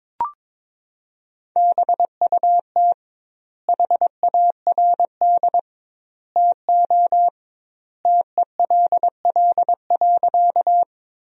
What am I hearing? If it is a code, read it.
Morse code, heard as a keyed tone, about 22 wpm, 700 Hz. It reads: BUT HARD TO TELL.
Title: 2x Word-Spacing - Sentences from Top 300 Words - 22wpm